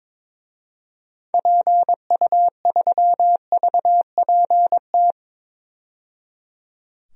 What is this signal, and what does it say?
Morse code, heard as a keyed tone, about 22 wpm, 700 Hz. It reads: PU3VPT
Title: DXCC-100 Call Signs - 6 Characters - 22wpm